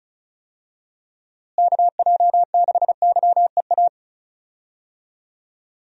Morse code, heard as a keyed tone, about 36 wpm, 700 Hz. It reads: KJ6YEA
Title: USA Call Signs - 6 Characters - Repeated 2x - 35wpm